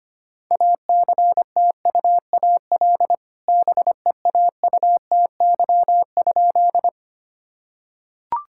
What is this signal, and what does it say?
Morse code, heard as a keyed tone, about 25 wpm, 700 Hz. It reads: ACTUAL BEAUTY?
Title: Sentences from Top 1000 Words - Review Part 5 - 25wpm